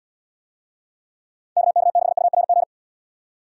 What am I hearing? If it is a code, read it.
Morse code, heard as a keyed tone, about 78 wpm, 700 Hz. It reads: KK6URG